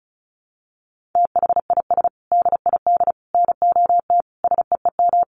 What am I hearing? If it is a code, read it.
Morse code, heard as a keyed tone, about 34 wpm, 700 Hz. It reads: THIS DID NOT SEEM
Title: Sentences from Top 1400 Words 35wpm